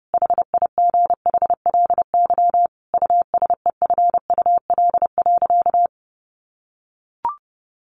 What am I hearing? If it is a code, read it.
Morse code, heard as a keyed tone, about 28 wpm, 700 Hz. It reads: HIGHLY USEFUL.
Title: Sentences from Top 1100 Words 30wpm